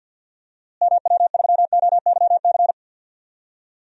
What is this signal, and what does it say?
Morse code, heard as a keyed tone, about 50 wpm, 700 Hz. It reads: MW3OYC
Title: DXCC-100 Call Signs - 6 Characters - Repeated 2x - 50wpm